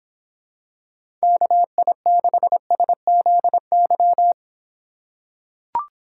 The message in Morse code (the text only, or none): KI6SZY